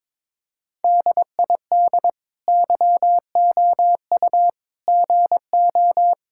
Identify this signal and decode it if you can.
Morse code, heard as a keyed tone, about 22 wpm, 700 Hz. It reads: DID YOU GO
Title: Sentences from Top 1000 Words - Review Part 5 - 22wpm